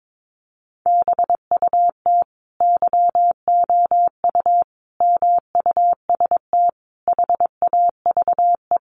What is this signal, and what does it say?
Morse code, heard as a keyed tone, about 22 wpm, 700 Hz. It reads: BUT YOU MUST HAVE
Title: Sentences from Top 1300 Words 22wpm